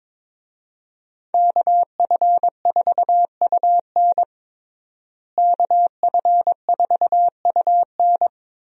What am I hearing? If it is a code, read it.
Morse code, heard as a keyed tone, about 22 wpm, 700 Hz. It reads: KF4UN KF4UN
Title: USA Call Signs - 5 Characters - Repeated 3x - 22wpm